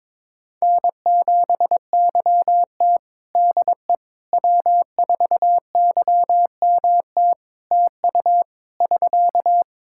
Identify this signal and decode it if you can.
Morse code, heard as a keyed tone, about 22 wpm, 700 Hz. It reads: N7YT DE W4YMT TU <SK>
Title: K1USN SST QSO Practice 22wpm